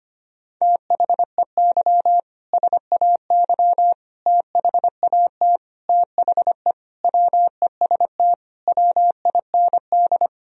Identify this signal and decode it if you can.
Morse code, heard as a keyed tone, about 25 wpm, 700 Hz. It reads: THEY SAY THAT THE WEST WIND
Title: Sentences from Top 1000 Words - Review Part 3 - 25wpm